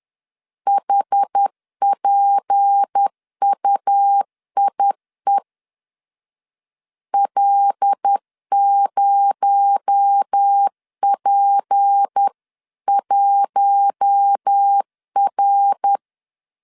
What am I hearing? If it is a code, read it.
Morse code, heard as a keyed tone, about 10 wpm, 800 Hz. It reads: HPUIE L0P1R
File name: CW / DET / 1 6 1500C S B 10wpm 4137